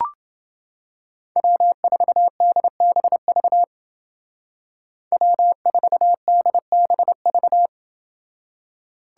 Morse code, heard as a keyed tone, about 29 wpm, 700 Hz. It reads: W4DBV W4DBV